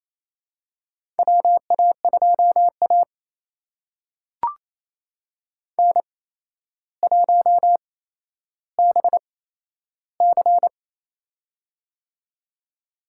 Morse code, heard as a keyed tone, about 28 wpm, spaced as for 12 wpm, 700 Hz. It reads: WA2A N1BC
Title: USA Call Signs - 4 Characters - 8x Character Spacing - 28wpm